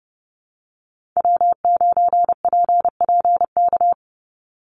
Morse code, heard as a keyed tone, about 29 wpm, 700 Hz. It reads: W9PPK
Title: USA Call Signs 30wpm